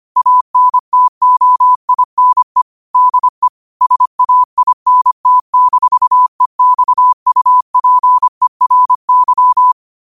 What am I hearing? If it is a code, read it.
Morse code, heard as a keyed tone, about 25 wpm, 1000 Hz. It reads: ANTOINE DE SAINT-EXUPERY
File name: qotdfiles / 25wpm